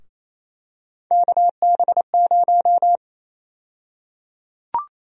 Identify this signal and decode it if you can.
Morse code, heard as a keyed tone, about 28 wpm, 700 Hz. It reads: KB0
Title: USA Call Sign Prefixes - Repeated 2x - 28wpm